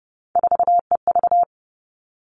Morse code, heard as a keyed tone, about 30 wpm, 700 Hz. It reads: 4EV
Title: All Cut Numbers - 3-Digit 30wpm